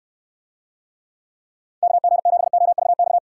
Morse code, heard as a keyed tone, about 69 wpm, 700 Hz. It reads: KG7OVX